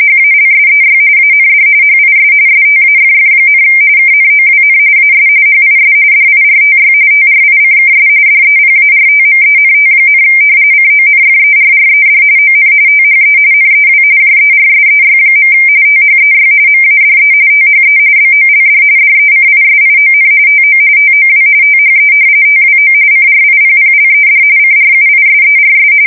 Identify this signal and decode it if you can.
RTTY, about 45.45 baud. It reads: HALSEY AND HIS STAFF, KNOWN AS HIS DIRTY TRICKS DEPARTMENT, CONFERRED ON THE NEW JERSEY. THEY MADE THE BOLD DECISION TO TOW THE TWO CRIPPLED CRUISERS
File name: fsk170 / battleshipnj